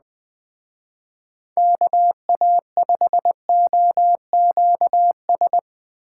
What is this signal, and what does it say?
Morse code, heard as a keyed tone, about 20 wpm, 700 Hz. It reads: KA5OQS